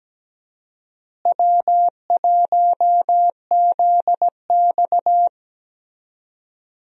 Morse code, heard as a keyed tone, about 17 wpm, 700 Hz. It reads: W1ZX
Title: USA Call Signs - 4 Characters - Repeated 2x - 17wpm